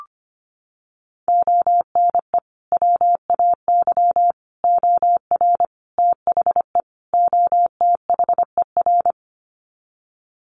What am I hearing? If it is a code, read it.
Morse code, heard as a keyed tone, about 25 wpm, 700 Hz. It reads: ONE WAY OR THE OTHER
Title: Sets of 5 Words 25wpm